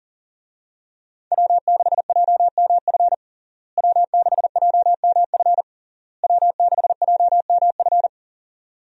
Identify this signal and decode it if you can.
Morse code, heard as a keyed tone, about 37 wpm, 700 Hz. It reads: W6JMF W6JMF W6JMF
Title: USA Call Signs - 5 Characters - Repeated 3x - 40wpm